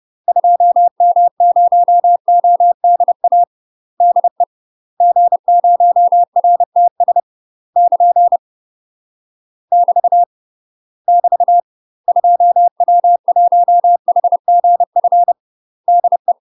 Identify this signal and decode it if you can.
Morse code, heard as a keyed tone, about 30 wpm, 700 Hz. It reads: 2M0ODA DE G0RTS <KN> <BT> <BT> 2W1HGF DE
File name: UK MASTER QSOs TRAINING / 30WPM / UK MASTER QSOs 0045